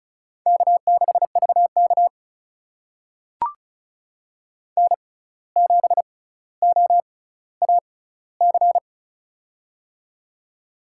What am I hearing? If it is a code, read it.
Morse code, heard as a keyed tone, about 36 wpm, spaced as for 17 wpm, 700 Hz. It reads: K6VK N7OAC
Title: USA Call Signs - 6x Character Spacing - 35wpm